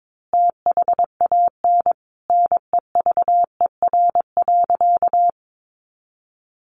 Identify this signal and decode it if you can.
Morse code, heard as a keyed tone, about 22 wpm, 700 Hz. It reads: THAN NEVER.
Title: Sentences from Top 400 Words 22wpm